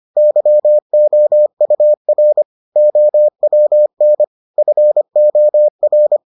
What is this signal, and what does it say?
Morse code, heard as a keyed tone, about 25 wpm, 600 Hz. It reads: YOUR OWN FOR